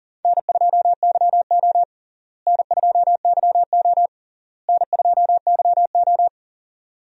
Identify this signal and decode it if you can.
Morse code, heard as a keyed tone, about 40 wpm, 700 Hz. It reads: N2YO N2YO N2YO
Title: USA Call Signs - 4 Characters - Repeated 3x - 40wpm